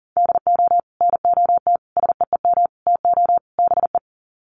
Morse code, heard as a keyed tone, about 40 wpm, 700 Hz. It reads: DO NOT SEEM TO BE
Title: Sets of 5 Words Encore2 40wpm